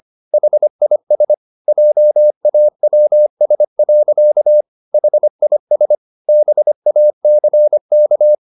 Morse code, heard as a keyed tone, about 25 wpm, 600 Hz. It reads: HIS JAWS. HIS BACK